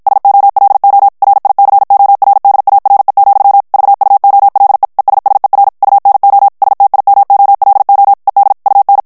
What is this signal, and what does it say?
Morse code, heard as a keyed tone, about 54 wpm, 800 Hz. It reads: SOLO RICORDARE, VUOLE ESSER UNO STIMOLO ED UN
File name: ITALIEN texte clair 1 vitesse 50 mots minute